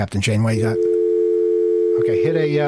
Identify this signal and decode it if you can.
Dial tone (North America), 350 + 440 Hz, with speech.
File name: mp3files / off the hook overtime -128